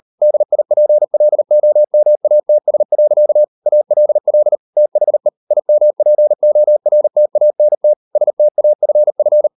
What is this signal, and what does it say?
Morse code, heard as a keyed tone, about 38 wpm, 600 Hz. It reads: DIPLOMATS. ALL THE IMPORTANT STAFF